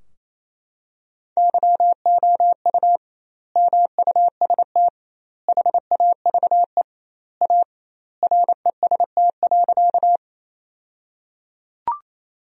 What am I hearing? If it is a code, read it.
Morse code, heard as a keyed tone, about 28 wpm, 700 Hz. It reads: YOU MUST HAVE A REST.
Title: 2x Word-Spacing - Sentences from Top 500 Words - 28wpm